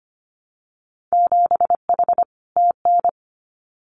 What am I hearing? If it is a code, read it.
Morse code, heard as a keyed tone, about 25 wpm, 700 Hz. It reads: 7H TN
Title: Winter Field Day Exchanges Encore 25wpm